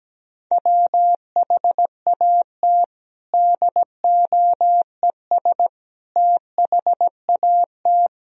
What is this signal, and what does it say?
Morse code, heard as a keyed tone, about 17 wpm, 700 Hz. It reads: WHAT DOES THAT
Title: Sentences from Top 1000 Words - Review Part 1 - 17wpm